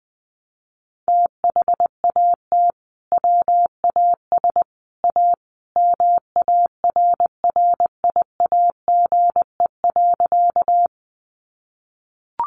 Morse code, heard as a keyed tone, about 20 wpm, 700 Hz. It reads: THAT WAS A MARRIAGE.
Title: Sentences from Top 800 Words 20wpm